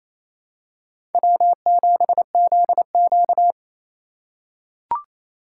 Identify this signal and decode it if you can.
Morse code, heard as a keyed tone, about 28 wpm, 700 Hz. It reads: W7ZQ
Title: USA Call Signs - 8x Character Spacing - 28wpm